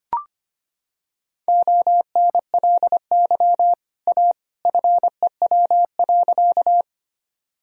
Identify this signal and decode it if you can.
Morse code, heard as a keyed tone, about 25 wpm, 700 Hz. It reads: ONLY A FEW.
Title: Sentences from Top 1000 Words - Review Part 1 - 25wpm